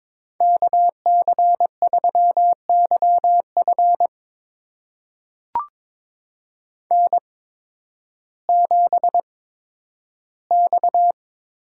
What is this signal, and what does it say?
Morse code, heard as a keyed tone, about 22 wpm, spaced as for 12 wpm, 700 Hz. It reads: KC3YF N7X
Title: USA Call Signs - 5 Characters - 8x Character Spacing - 22wpm